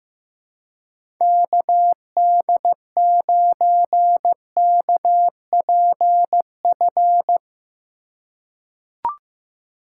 Morse code, heard as a keyed tone, about 15 wpm, 700 Hz. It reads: KD9KPF
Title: USA Call Signs - 6 Characters - 2x Character Spacing - 15wpm